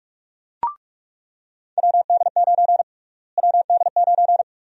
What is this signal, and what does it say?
Morse code, heard as a keyed tone, about 46 wpm, 700 Hz. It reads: WD9 WD9
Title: USA Call Sign Prefixes - Repeated 2x - 45wpm